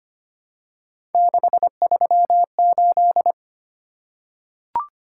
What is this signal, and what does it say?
Morse code, heard as a keyed tone, about 25 wpm, 700 Hz. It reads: 638
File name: Three Letter-Number Combinations - 25wpm